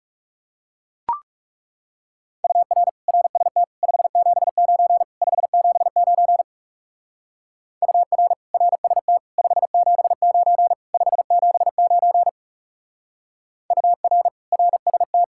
Morse code, heard as a keyed tone, about 36 wpm, 700 Hz. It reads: UR RST 579 579 UR RST 579 579 UR RST
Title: Speed-Racing Common QSO Phrases Encore - 30wpm